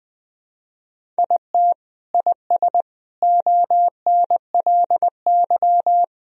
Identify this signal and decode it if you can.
Morse code, heard as a keyed tone, about 20 wpm, 700 Hz. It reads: IT IS ONLY
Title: Sentences from Top 1000 Words - Review Part 1 - 20wpm